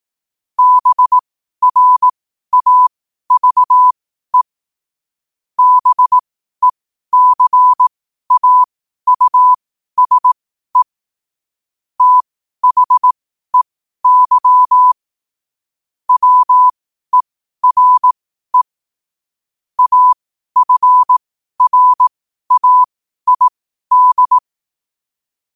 Morse code, heard as a keyed tone, about 18 wpm, spaced as for 12 wpm, 1000 Hz. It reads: BRAVE BECAUSE THEY WERE AFRAID
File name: qotdfiles / 12wpm